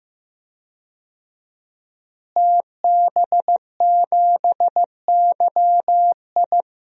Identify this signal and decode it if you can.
Morse code, heard as a keyed tone, about 15 wpm, 700 Hz. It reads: TB7YI
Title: DXCC-100 Call Signs - 5 Characters - Repeated 2x - 15wpm